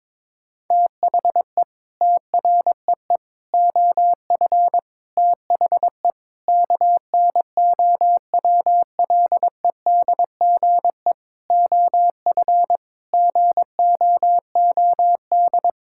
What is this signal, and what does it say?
Morse code, heard as a keyed tone, about 22 wpm, 700 Hz. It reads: THE TREE OF THE KNOWLEDGE OF GOOD
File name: Sentences with 9-Word Phrases - 22wpm